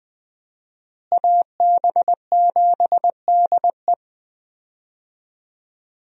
Morse code, heard as a keyed tone, about 20 wpm, 700 Hz. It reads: AB7DE
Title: USA Call Signs - 5 Characters - Repeated 3x - 20wpm